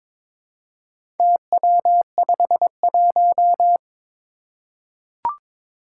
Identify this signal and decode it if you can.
Morse code, heard as a keyed tone, about 22 wpm, 700 Hz. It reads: TW51
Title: Four Challenging Letter-Number Combinations 22wpm